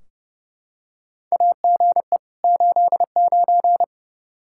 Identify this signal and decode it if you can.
Morse code, heard as a keyed tone, about 30 wpm, 700 Hz. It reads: AGE 89
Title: K1USN SST QSO Practice 30wpm